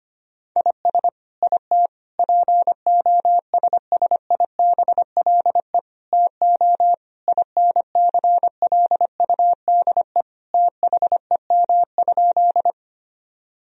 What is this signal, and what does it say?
Morse code, heard as a keyed tone, about 25 wpm, 700 Hz. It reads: IS IT POSSIBLE TO INCLUDE THEM?
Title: Sentences from Top 1000 Words - Review Part 3 - 25wpm